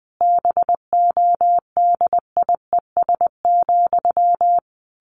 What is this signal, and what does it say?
Morse code, heard as a keyed tone, about 20 wpm, 700 Hz. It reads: BODIES,
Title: A Sound Of Thunder 20wpm